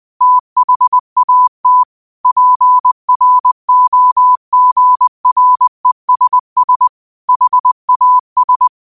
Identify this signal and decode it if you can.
Morse code, heard as a keyed tone, about 20 wpm, 1000 Hz. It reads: THAT PROGRESS HAS
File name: qotdfiles / 20wpm